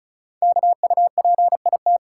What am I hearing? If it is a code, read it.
Morse code, heard as a keyed tone, about 36 wpm, 700 Hz. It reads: KUPIT
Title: Code Groups - 5 Letter:Numbers - 35wpm